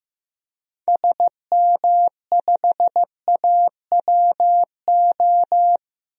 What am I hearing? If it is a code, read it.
Morse code, heard as a keyed tone, about 15 wpm, 700 Hz. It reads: SM5AWO
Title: DXCC-100 Call Signs - 6 Characters - 15wpm